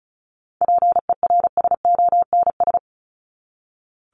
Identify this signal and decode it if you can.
Morse code, heard as a keyed tone, about 34 wpm, 700 Hz. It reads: PERSONS
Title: ES:S Suffix - Top 100 Words with Base Words 35wpm